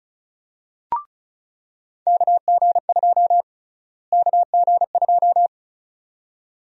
Morse code, heard as a keyed tone, about 36 wpm, 700 Hz. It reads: KG2 KG2
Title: USA Call Sign Prefixes - Repeated 2x - 35wpm